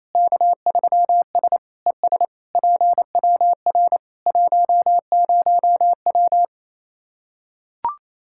Morse code, heard as a keyed tone, about 28 wpm, 700 Hz. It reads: K3S ES PWR 10W
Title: Practice QSO - Exchange 2 - Info - 28wpm